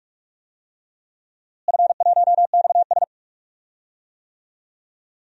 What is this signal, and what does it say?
Morse code, heard as a keyed tone, about 43 wpm, 700 Hz. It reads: F1XS
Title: DXCC-100 Call Signs - 4 Characters - Repeated 2x - 45wpm